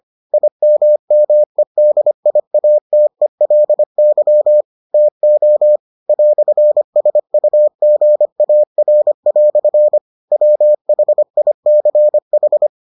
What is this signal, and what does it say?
Morse code, heard as a keyed tone, about 25 wpm, 600 Hz. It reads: IMMEDIATELY TO "SUGAR" WHICH